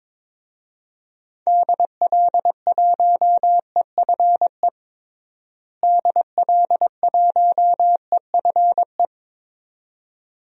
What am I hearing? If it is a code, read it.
Morse code, heard as a keyed tone, about 22 wpm, 700 Hz. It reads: DL1EFE DL1EFE
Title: DXCC-100 Call Signs - 6 Characters - Repeated 2x - 22wpm